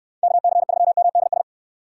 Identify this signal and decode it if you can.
Morse code, heard as a keyed tone, about 69 wpm, 700 Hz. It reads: KC3MDS